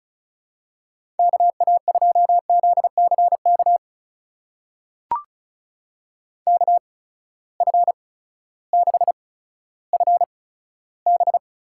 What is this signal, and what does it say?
Morse code, heard as a keyed tone, about 36 wpm, spaced as for 16 wpm, 700 Hz. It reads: KA2ZCK KF6FB
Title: USA Call Signs - 6 Characters - 8x Character Spacing - 35wpm